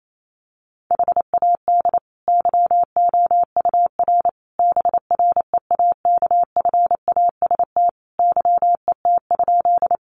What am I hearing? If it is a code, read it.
Morse code, heard as a keyed tone, about 28 wpm, 700 Hz. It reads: HAD YOUR BREAKFAST YET?
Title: Sentences from Top 1500 Words 28wpm